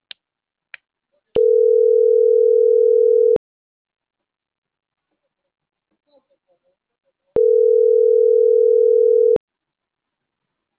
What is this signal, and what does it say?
Ringback tone (North America), 440 + 480 Hz.